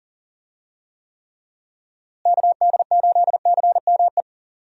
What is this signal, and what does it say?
Morse code, heard as a keyed tone, about 40 wpm, 700 Hz. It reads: KD8CME
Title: USA Call Signs - 6 Characters - Repeated 3x - 40wpm